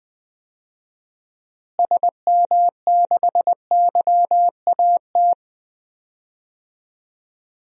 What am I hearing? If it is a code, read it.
Morse code, heard as a keyed tone, about 20 wpm, 700 Hz. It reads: SM6YAT
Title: DXCC-100 Call Signs - 6 Characters - Repeated 2x - 20wpm